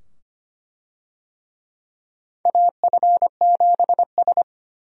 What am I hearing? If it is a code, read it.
Morse code, heard as a keyed tone, about 25 wpm, 700 Hz. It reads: AF7S
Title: USA Call Signs - 4 Characters - 25wpm